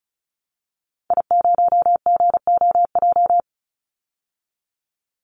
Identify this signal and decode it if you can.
Morse code, heard as a keyed tone, about 34 wpm, 700 Hz. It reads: I0GOJ
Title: Call Signs Repeated 2x - 35wpm